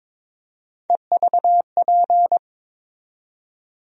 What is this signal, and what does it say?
Morse code, heard as a keyed tone, about 22 wpm, 700 Hz. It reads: EVP
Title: USA Call Sign Postfixes - 3 Letters 22wpm